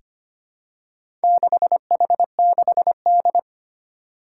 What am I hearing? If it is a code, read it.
Morse code, heard as a keyed tone, about 25 wpm, 700 Hz. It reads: 6H6D
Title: Four Challenging Letter-Number Combinations 25wpm